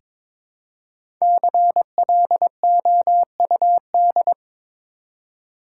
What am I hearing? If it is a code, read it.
Morse code, heard as a keyed tone, about 22 wpm, 700 Hz. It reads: CLOUD
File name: 5 Letter Words - First 3 Letters Given - 22wpm